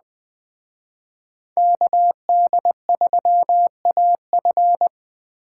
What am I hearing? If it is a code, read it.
Morse code, heard as a keyed tone, about 20 wpm, 700 Hz. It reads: KD3AF
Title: Speed Racing - USA 2x2 Call Signs 17wpm